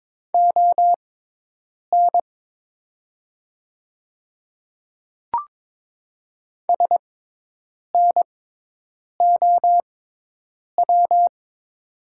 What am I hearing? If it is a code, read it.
Morse code, heard as a keyed tone, about 22 wpm, spaced as for 8 wpm, 700 Hz. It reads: ON SNOW
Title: 6x Character Spacing - Top 500 Words - 22wpm